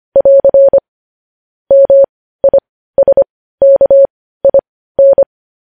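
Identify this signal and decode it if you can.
Morse code, heard as a keyed tone, about 25 wpm, spaced as for 15 wpm, 550 Hz. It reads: <AR> MISKIN
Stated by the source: News headlines sent in Morse code at 15 wpm.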